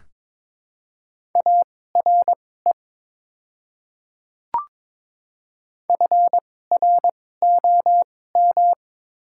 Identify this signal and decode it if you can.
Morse code, heard as a keyed tone, about 22 wpm, spaced as for 16 wpm, 700 Hz. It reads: ARE FROM
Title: 2x Character Spacing - Top 100 Words - 22wpm